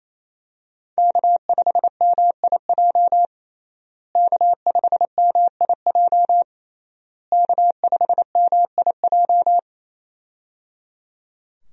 Morse code, heard as a keyed tone, about 28 wpm, 700 Hz. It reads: K5MIJ K5MIJ K5MIJ
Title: USA Call Signs - 5 Characters - Repeated 3x - 28wpm